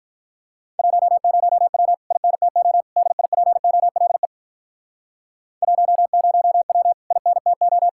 Morse code, heard as a keyed tone, about 45 wpm, 700 Hz. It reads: 10W INTO DIPOLE 10W INTO